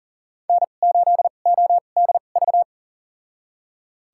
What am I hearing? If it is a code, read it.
Morse code, heard as a keyed tone, about 40 wpm, spaced as for 29 wpm, 700 Hz. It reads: N8ODV
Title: USA Call Signs - 5 Characters - 2x Character Spacing - 40wpm